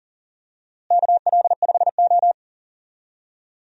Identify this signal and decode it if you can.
Morse code, heard as a keyed tone, about 40 wpm, 700 Hz. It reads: KL5O
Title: USA Call Signs - 4 Characters - Repeated 3x - 40wpm